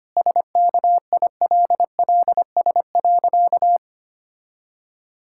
Morse code, heard as a keyed tone, about 25 wpm, 700 Hz. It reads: SKILLS.
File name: practice / 25wpm / Sentences with 2-Word Phrases - 25wpm